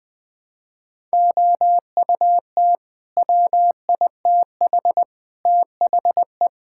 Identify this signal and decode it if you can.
Morse code, heard as a keyed tone, about 20 wpm, 700 Hz. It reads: OUT WITH THE
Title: Sentences with 8-Word Phrases - 20wpm